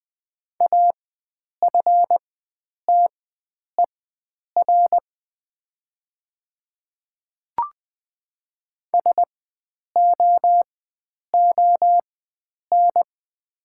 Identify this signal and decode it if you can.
Morse code, heard as a keyed tone, about 20 wpm, spaced as for 9 wpm, 700 Hz. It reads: AFTER SOON
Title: 4x Character Spacing - Top 500 Words - 20wpm